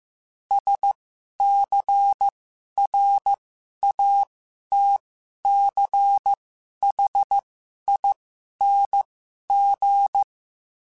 Morse code, heard as a keyed tone, about 15 wpm, spaced as for 11 wpm, 800 Hz. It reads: SCRATCHING
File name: Special INGs-mp3s / Special INGs-mp3s / Sngl Ing @10